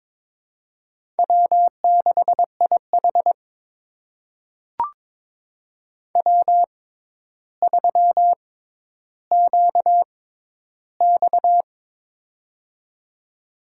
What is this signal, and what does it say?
Morse code, heard as a keyed tone, about 22 wpm, spaced as for 11 wpm, 700 Hz. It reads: W6IH W3QX
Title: USA Call Signs - 4 Characters - 6x Character Spacing - 22wpm